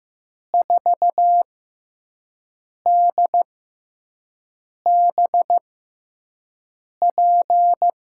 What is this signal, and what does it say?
Morse code, heard as a keyed tone, about 15 wpm, spaced as for 5 wpm, 700 Hz. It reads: 4DBP
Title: USA Call Signs - 6x Character Spacing - 15wpm